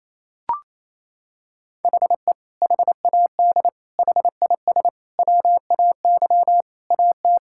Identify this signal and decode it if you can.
Morse code, heard as a keyed tone, about 28 wpm, 700 Hz. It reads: HE HAD HIS WAY AT
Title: Sentences from Top 1600 Words 28wpm